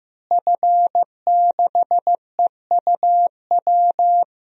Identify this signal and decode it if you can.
Morse code, heard as a keyed tone, about 15 wpm, 700 Hz. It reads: F6EUW
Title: DXCC-100 Call Signs - 5 Characters - Repeated 2x - 15wpm